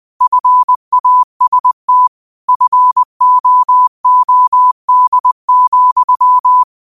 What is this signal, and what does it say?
Morse code, heard as a keyed tone, about 20 wpm, 1000 Hz. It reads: FAST FOOD,